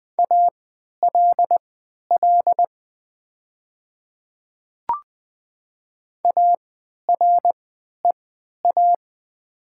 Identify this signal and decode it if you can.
Morse code, heard as a keyed tone, about 20 wpm, spaced as for 11 wpm, 700 Hz. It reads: ALL AREA
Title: 3x Character Spacing - Top 500 Words - 20wpm